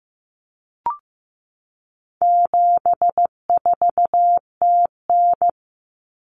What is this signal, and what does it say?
Morse code, heard as a keyed tone, about 15 wpm, 700 Hz. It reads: 74TN
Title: Common Cut Numbers - 4-Digit 15wpm